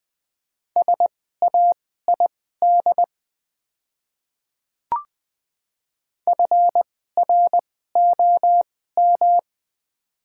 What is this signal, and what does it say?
Morse code, heard as a keyed tone, about 20 wpm, spaced as for 14 wpm, 700 Hz. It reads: SAID FROM
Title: 2x Character Spacing - Top 100 Words - 20wpm